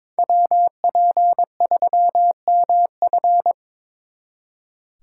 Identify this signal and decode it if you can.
Morse code, heard as a keyed tone, about 26 wpm, 700 Hz. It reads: WP3MF
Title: Speed Racing - USA 2x2 Call Signs 22wpm